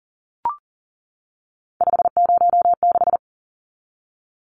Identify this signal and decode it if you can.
Morse code, heard as a keyed tone, about 40 wpm, 700 Hz. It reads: 506
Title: Numbers - 3-Digit 40wpm